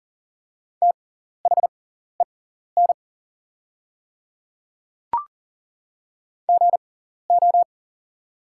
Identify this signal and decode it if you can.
Morse code, heard as a keyed tone, about 40 wpm, spaced as for 14 wpm, 700 Hz. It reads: THEN GO
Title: 6x Character Spacing - Top 100 Words - 40wpm